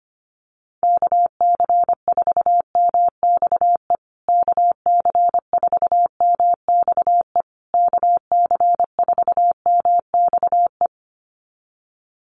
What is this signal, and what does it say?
Morse code, heard as a keyed tone, about 25 wpm, 700 Hz. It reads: KC4MXE KC4MXE KC4MXE
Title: USA Call Signs Repeated 3x 25wpm